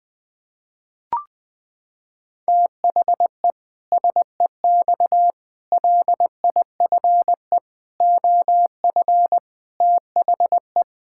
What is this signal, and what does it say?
Morse code, heard as a keyed tone, about 20 wpm, 700 Hz. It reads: THE SEX LIFE OF THE
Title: Pangram Challenge v2 - 20wpm